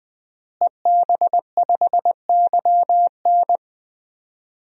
Morse code, heard as a keyed tone, about 20 wpm, 700 Hz. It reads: EB5YN
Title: DXCC-100 Call Signs - 5 Characters - Repeated 2x - 20wpm